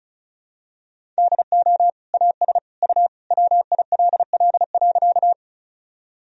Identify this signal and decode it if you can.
Morse code, heard as a keyed tone, about 36 wpm, 700 Hz. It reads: DO AS U WILL.
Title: Sentences from Top 100 Words and Abbreviations 35wpm